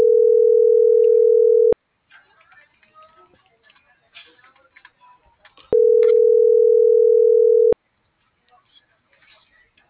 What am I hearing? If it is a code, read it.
Ringback tone (North America), 440 + 480 Hz.